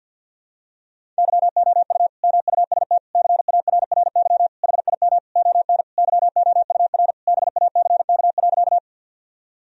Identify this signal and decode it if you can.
Morse code, heard as a keyed tone, about 52 wpm, 700 Hz. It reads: YOU MUST CARRY HIM ON YOUR BACK.
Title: Sentences from Top 1000 Words - Review Part 1 - 50wpm